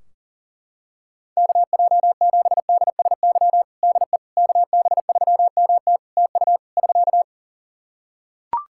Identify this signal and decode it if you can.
Morse code, heard as a keyed tone, about 40 wpm, 700 Hz. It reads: KJ7DSY DE KB3MT TU <SK>
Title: K1USN SST QSO Practice 40wpm